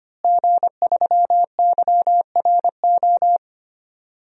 Morse code, heard as a keyed tone, about 25 wpm, 700 Hz. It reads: G3YRO
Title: DXCC-100 Call Signs - 5 Characters - Repeated 3x - 25wpm